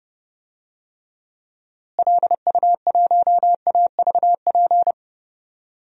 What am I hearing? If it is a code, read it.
Morse code, heard as a keyed tone, about 30 wpm, 700 Hz. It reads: LU1AVP
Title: DXCC-100 Call Signs - 6 Characters - Repeated 3x - 30wpm